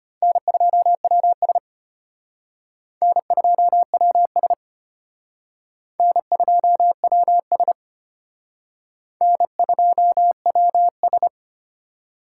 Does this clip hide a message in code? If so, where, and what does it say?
Yes: Morse code from 0.2 to 11.3 s, heard as a keyed tone, about 32 wpm, 700 Hz. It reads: N2WS N2WS N2WS N2WS